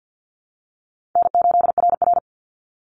Morse code, heard as a keyed tone, about 50 wpm, 700 Hz. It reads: N8RR
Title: Easy Call Signs Repeated 2x - 50wpm